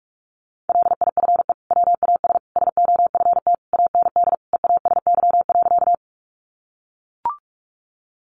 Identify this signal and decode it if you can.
Morse code, heard as a keyed tone, about 44 wpm, 700 Hz. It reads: LIFE WAS SOFT AND EASY.
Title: Sentences from Top 1500 Words 45wpm